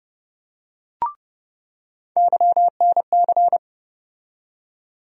Morse code, heard as a keyed tone, about 30 wpm, 700 Hz. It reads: YNC
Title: USA Call Sign Postfixes - 3 Letters 30wpm